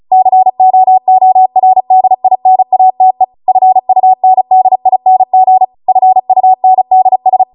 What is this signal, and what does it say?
Morse code, heard as a keyed tone, about 35 wpm, 750 Hz. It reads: COORDINATE FUNDING FUNDS
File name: Morse / 35WPM